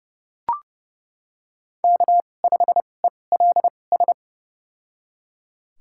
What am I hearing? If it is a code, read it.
Morse code, heard as a keyed tone, about 30 wpm, spaced as for 22 wpm, 700 Hz. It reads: K5ELS
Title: USA Call Signs - 5 Characters - 2x Character Spacing - 30wpm